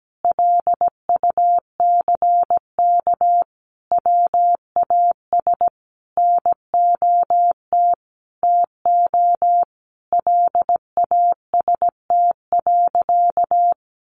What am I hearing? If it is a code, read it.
Morse code, heard as a keyed tone, about 17 wpm, 700 Hz. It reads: LUCK WAS NOT TO LAST.
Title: Sentences from Top 1500 Words 17wpm